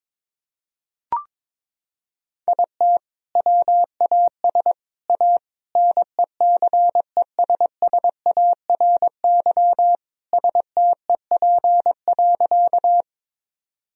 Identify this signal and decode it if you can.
Morse code, heard as a keyed tone, about 22 wpm, 700 Hz. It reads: IT WAS A NECESSARY STEP.
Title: Sentences from Top 1000 Words - Review Part 2 - 22wpm